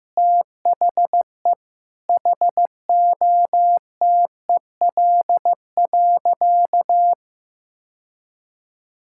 Morse code, heard as a keyed tone, about 15 wpm, 700 Hz. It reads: THE HOTEL.
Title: Sentences from Top 1600 Words 15wpm